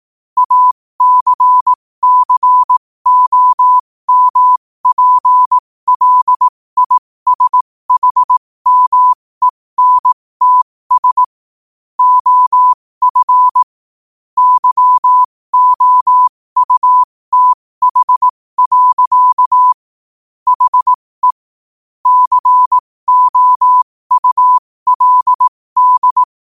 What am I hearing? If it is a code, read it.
Morse code, heard as a keyed tone, about 18 wpm, spaced as for 15 wpm, 1000 Hz. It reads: ACCOMPLISHMENTS OF YOUTH. HE COULD
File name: qotdfiles / 15wpm